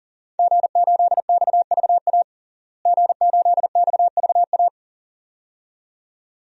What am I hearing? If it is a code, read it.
Morse code, heard as a keyed tone, about 40 wpm, 700 Hz. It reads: G8XVA G8XVA
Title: DXCC-100 Call Signs - 5 Characters - Repeated 2x - 40wpm